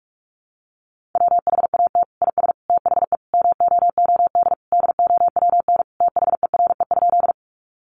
Morse code, heard as a keyed tone, about 44 wpm, 700 Hz. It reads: WHAT IS THE MOOD DOWN THERE?
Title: Sentences from Top 1100 Words 45wpm